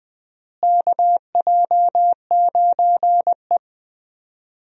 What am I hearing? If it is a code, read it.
Morse code, heard as a keyed tone, about 21 wpm, 700 Hz. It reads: KJ9E
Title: USA Call Signs - 4 Characters - Repeated 3x - 20wpm